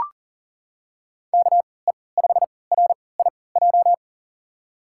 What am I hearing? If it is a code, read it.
Morse code, heard as a keyed tone, about 40 wpm, spaced as for 23 wpm, 700 Hz. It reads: KE5RIJ
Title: USA Call Signs - 6 Characters - 3x Character Spacing - 40wpm